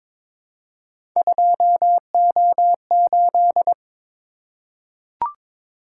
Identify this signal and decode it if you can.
Morse code, heard as a keyed tone, about 22 wpm, 700 Hz. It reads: 2O8
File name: practice / 22wpm / Three Letter-Number Combinations - 22wpm